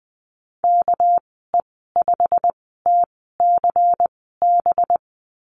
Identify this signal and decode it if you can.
Morse code, heard as a keyed tone, about 20 wpm, spaced as for 15 wpm, 700 Hz. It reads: KE5TCB
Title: 207 Lesson 33g - Letter j - Call Signs 20wpm